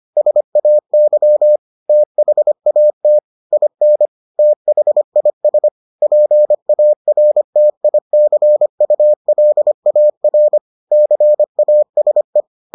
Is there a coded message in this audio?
Yes: Morse code from 0.2 to 12.4 s, heard as a keyed tone, about 25 wpm, 600 Hz. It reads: SAY THAT IN THIS PARTICULAR CASE